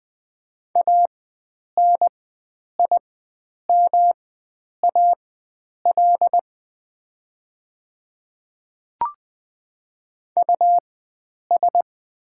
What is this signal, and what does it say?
Morse code, heard as a keyed tone, about 20 wpm, spaced as for 9 wpm, 700 Hz. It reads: ANIMAL US
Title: 4x Character Spacing - Top 300 Words - 20wpm